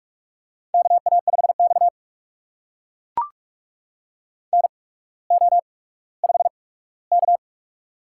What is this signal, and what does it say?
Morse code, heard as a keyed tone, about 44 wpm, spaced as for 19 wpm, 690 Hz. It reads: KA5X NO5K
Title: USA Call Signs - 4 Characters - 8x Character Spacing - 45wpm